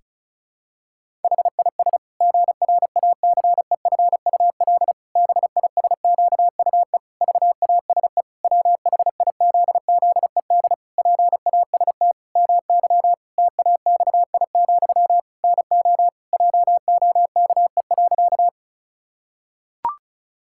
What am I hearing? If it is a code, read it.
Morse code, heard as a keyed tone, about 34 wpm, 700 Hz. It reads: HIS GRACEFUL BISQUE VASE WHIZZED PAST MY TAXI, NO JOKE.
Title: Pangram Challenge v2 - 35wpm